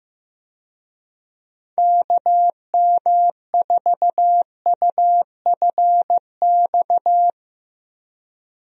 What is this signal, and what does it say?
Morse code, heard as a keyed tone, about 15 wpm, 700 Hz. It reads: KM4UFX
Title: USA Call Signs - 6 Characters - Repeated 2x - 15wpm